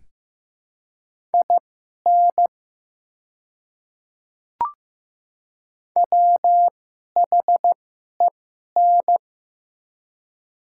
Morse code, heard as a keyed tone, about 15 wpm, spaced as for 11 wpm, 700 Hz. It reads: IN WHEN
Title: 2x Character Spacing - Top 100 Words - 15wpm